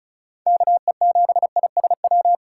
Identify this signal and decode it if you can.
Morse code, heard as a keyed tone, about 36 wpm, 700 Hz. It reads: KE7ISW
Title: USA Call Signs - 6 Characters - 4x Character Spacing - 35wpm